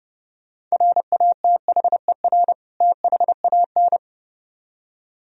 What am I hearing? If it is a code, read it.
Morse code, heard as a keyed tone, about 36 wpm, 700 Hz. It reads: RATHER THAN
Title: Speed-Racing - Sets of 2 Words 30wpm